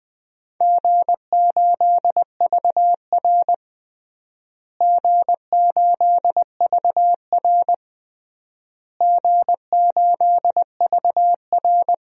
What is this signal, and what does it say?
Morse code, heard as a keyed tone, about 20 wpm, 700 Hz. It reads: G8VR G8VR G8VR
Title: DXCC-100 Call Signs - 4 Characters - Repeated 3x - 20wpm